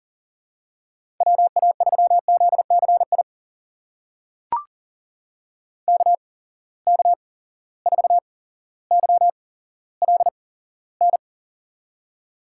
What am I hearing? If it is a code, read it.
Morse code, heard as a keyed tone, about 38 wpm, spaced as for 17 wpm, 700 Hz. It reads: WA3ZCI KK4YLN
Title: USA Call Signs - 6 Characters - 8x Character Spacing - 40wpm